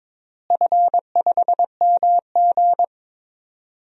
Morse code, heard as a keyed tone, about 22 wpm, 700 Hz. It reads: F5MG
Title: DXCC-100 Call Signs - 4 Characters - Repeated 2x - 22wpm